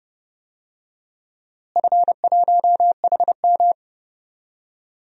Morse code, heard as a keyed tone, about 32 wpm, 700 Hz. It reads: F1HM
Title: DXCC-100 Call Signs - 4 Characters - Repeated 2x - 30wpm